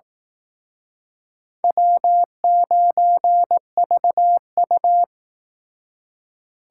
Morse code, heard as a keyed tone, about 19 wpm, 700 Hz. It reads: W9VU
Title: Speed Racing - USA 1x2 Call Signs 15wpm